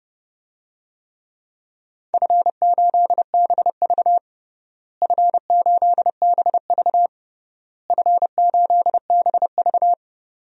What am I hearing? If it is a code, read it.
Morse code, heard as a keyed tone, about 32 wpm, 700 Hz. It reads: F8BV F8BV F8BV
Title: DXCC-100 Call Signs - 4 Characters - Repeated 3x - 30wpm